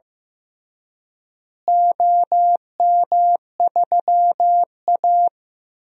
Morse code, heard as a keyed tone, about 15 wpm, 700 Hz. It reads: OM3A
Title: DXCC-100 Call Signs - 4 Characters - Repeated 3x - 15wpm